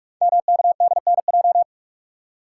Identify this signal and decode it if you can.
Morse code, heard as a keyed tone, about 45 wpm, 690 Hz. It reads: MKDNJ
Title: Code Groups - 5 Letters - 45wpm